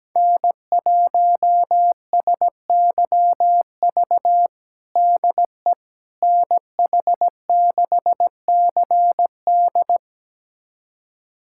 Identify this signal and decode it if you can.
Morse code, heard as a keyed tone, about 17 wpm, 700 Hz. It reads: N1SYV DE NH6CD
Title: USA Callsign DE Callsign - 5 Characters - 17wpm